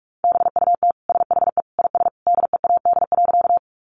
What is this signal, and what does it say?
Morse code, heard as a keyed tone, about 44 wpm, 700 Hz. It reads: BUT SHE IS DEAD.
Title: Sentences from Top 1500 Words 45wpm